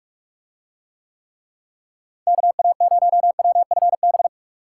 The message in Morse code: KA0WFB

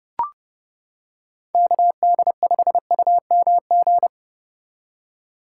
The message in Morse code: KD5UMG

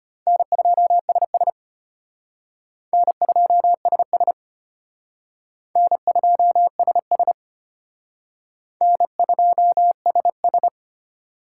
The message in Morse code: N2SS N2SS N2SS N2SS

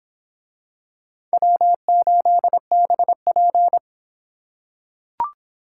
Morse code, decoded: W8BP

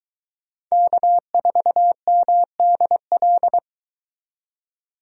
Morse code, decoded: K4MDL

